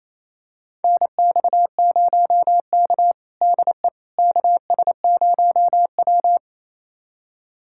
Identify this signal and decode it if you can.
Morse code, heard as a keyed tone, about 28 wpm, 700 Hz. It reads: NX0K DE KS0W
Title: USA Callsign DE Callsign - 4 Characters - 28wpm